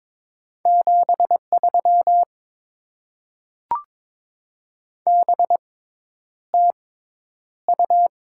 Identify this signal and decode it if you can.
Morse code, heard as a keyed tone, about 22 wpm, spaced as for 10 wpm, 690 Hz. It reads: 73 BTU
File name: Top 20 QSO Elements - 6x Character Spacing - 22wpm